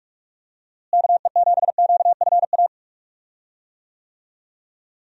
Morse code, heard as a keyed tone, about 46 wpm, 700 Hz. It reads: KE7QFA
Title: USA Call Signs - 6 Characters - Repeated 2x - 45wpm